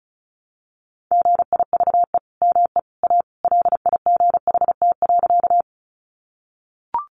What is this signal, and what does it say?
Morse code, heard as a keyed tone, about 34 wpm, 700 Hz. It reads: GIVE ME A LIGHT.
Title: Sentences from Top 200 Words 35wpm